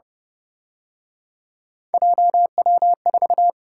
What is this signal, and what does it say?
Morse code, heard as a keyed tone, about 31 wpm, 700 Hz. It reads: JW4
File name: Speed-Racing - Three Letter-Number Combinations - 25wpm